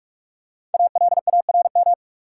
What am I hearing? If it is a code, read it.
Morse code, heard as a keyed tone, about 47 wpm, 700 Hz. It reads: ALARM
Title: Five Letter Words - 45wpm